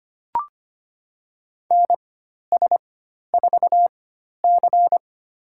Morse code, heard as a keyed tone, about 25 wpm, spaced as for 12 wpm, 700 Hz. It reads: NS4C
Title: USA Call Signs - 4 Characters - 4x Character Spacing - 25wpm